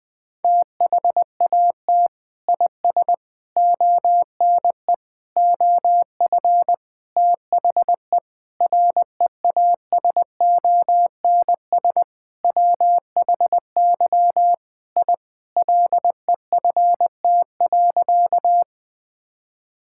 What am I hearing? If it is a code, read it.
Morse code, heard as a keyed tone, about 20 wpm, 700 Hz. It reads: THAT IS ONE OF THE REASONS WHY I LEFT.
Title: Sentences with 5-Word Phrases - 20wpm